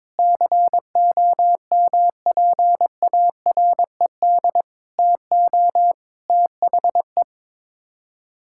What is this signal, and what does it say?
Morse code, heard as a keyed tone, about 22 wpm, 700 Hz. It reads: COMPARED TO THE
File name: practice / 22wpm / Sets of 8 Words - 22wpm